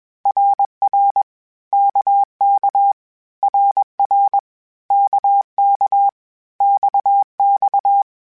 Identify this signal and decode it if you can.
Morse code, heard as a keyed tone, about 21 wpm, 800 Hz. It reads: RR KK RR KK XX
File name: Groups@20 / Groups@20 mp3s / RKPX@20